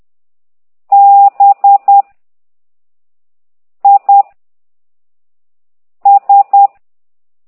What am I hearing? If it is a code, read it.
Morse code, heard as a keyed tone, about 10 wpm, 800 Hz. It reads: B I S
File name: exerc7